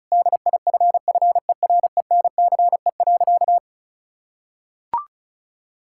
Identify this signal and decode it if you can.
Morse code, heard as a keyed tone, about 34 wpm, 700 Hz. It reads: DIFFERENCE.
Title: Sentences from Top 1000 Words - Review Part 5 - 35wpm